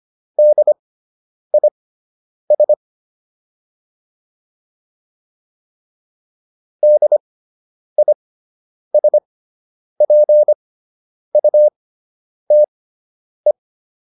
Morse code, heard as a keyed tone, about 25 wpm, spaced as for 9 wpm, 600 Hz. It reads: DIS DISPUTE